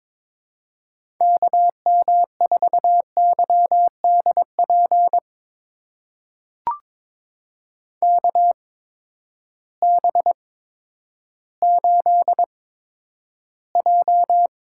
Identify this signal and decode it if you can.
Morse code, heard as a keyed tone, about 22 wpm, spaced as for 11 wpm, 700 Hz. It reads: KM4YDP KB8J
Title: USA Call Signs - 6 Characters - 8x Character Spacing - 22wpm